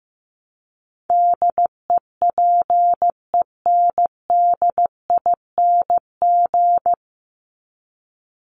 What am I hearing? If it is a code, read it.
Morse code, heard as a keyed tone, about 15 wpm, 700 Hz. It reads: DEPENDING
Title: ING Suffix - Top 100 Words with Base Words 15wpm